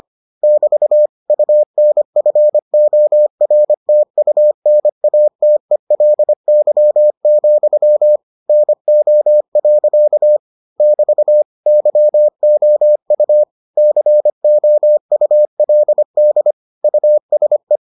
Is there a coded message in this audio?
Yes: Morse code from 0.4 to 17.8 s, heard as a keyed tone, about 25 wpm, 600 Hz. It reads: <BT> UNFORTUNATELY, NO. <BT> YOU COULD USE